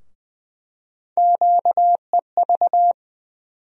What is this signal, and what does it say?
Morse code, heard as a keyed tone, about 20 wpm, 700 Hz. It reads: QEV
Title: USA Call Sign Postfixes - 3 Letters 20wpm